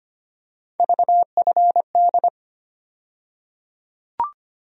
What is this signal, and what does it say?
Morse code, heard as a keyed tone, about 25 wpm, 700 Hz. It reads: VFD
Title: USA Call Sign Postfixes - 3 Letters 25wpm